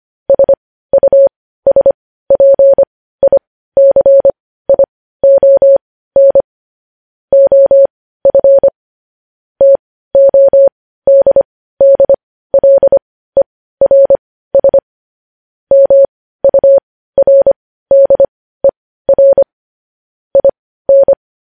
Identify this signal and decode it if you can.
Morse code, heard as a keyed tone, about 25 wpm, spaced as for 15 wpm, 550 Hz. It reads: SUSPICION OF TODDLERS MURDER IN
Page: news-headlines-in-morse-code-at-15-wpm